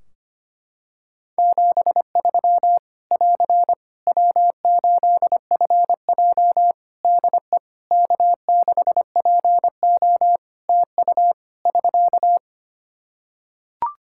Morse code, heard as a keyed tone, about 25 wpm, 700 Hz. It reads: 73 <AR> W8FJ DE K6PO TU <SK>
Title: Practice QSO - Exchange 4 - Ending - 25wpm